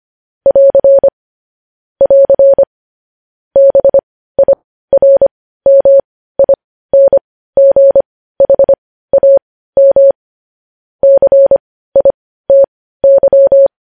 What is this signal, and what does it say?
Morse code, heard as a keyed tone, about 25 wpm, spaced as for 15 wpm, 550 Hz. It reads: <AR> <AR> BIRMINGHAM CITY
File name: podcast / News headlines-1x-25x15